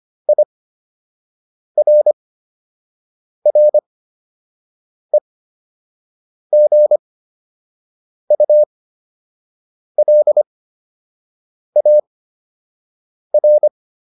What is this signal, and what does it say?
Morse code, heard as a keyed tone, about 25 wpm, spaced as for 6 wpm, 600 Hz. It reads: IRREGULAR